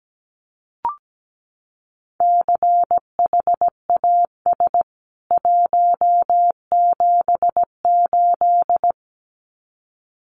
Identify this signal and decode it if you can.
Morse code, heard as a keyed tone, about 17 wpm, 700 Hz. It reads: CHAS 178
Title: CWT Exchange 17wpm